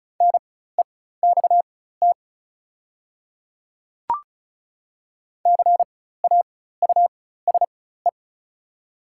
Morse code, heard as a keyed tone, about 34 wpm, spaced as for 16 wpm, 690 Hz. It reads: NEXT CAUSE
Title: 4x Character Spacing - Top 500 Words - 35wpm